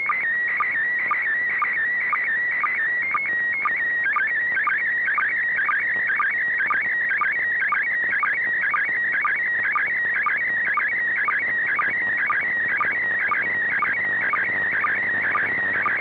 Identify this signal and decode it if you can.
SSTV (slow-scan television), PD 120.
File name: ik1yrc / AUDIO FOR QRZ / END ISS 29 01 2021 05 00 00 UTC